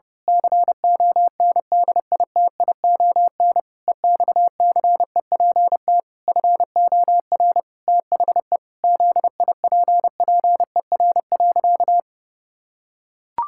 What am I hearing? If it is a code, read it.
Morse code, heard as a keyed tone, about 30 wpm, 700 Hz. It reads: CONDITION EXCEPT FOR THE ZIPPER.
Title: Pangram Challenge v2 - 30wpm